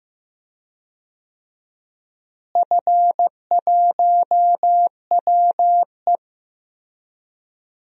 Morse code, heard as a keyed tone, about 15 wpm, 700 Hz. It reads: F1WE
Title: DXCC-100 Call Signs - 4 Characters - Repeated 2x - 15wpm